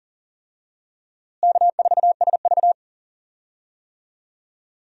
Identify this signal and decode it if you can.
Morse code, heard as a keyed tone, about 39 wpm, 700 Hz. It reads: K4SV
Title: USA Call Signs - 4 Characters - Repeated 2x - 40wpm